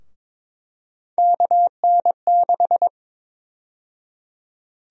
Morse code, heard as a keyed tone, about 22 wpm, 700 Hz. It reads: KN6
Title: USA Call Sign Prefixes - 22wpm